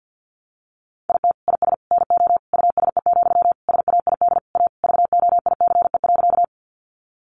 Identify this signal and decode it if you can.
Morse code, heard as a keyed tone, about 50 wpm, 700 Hz. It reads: IT IS NO USE, SAID A VOICE.
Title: Sentences from Top 500 Words 50wpm